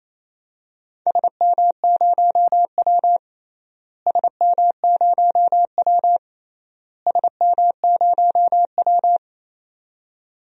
Morse code, heard as a keyed tone, about 28 wpm, 700 Hz. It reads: SM0W SM0W SM0W
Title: DXCC-100 Call Signs - 4 Characters - Repeated 3x - 28wpm